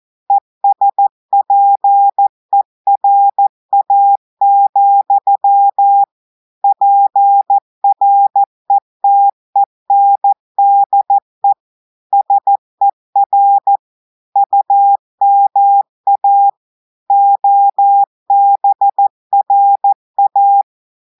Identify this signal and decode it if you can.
Morse code, heard as a keyed tone, about 14 wpm, 800 Hz. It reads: ESPERA, PRETENDE SER UMA OBRA
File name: audios / portugais / portugais texte en clair 1 vitesse 14 mots minutes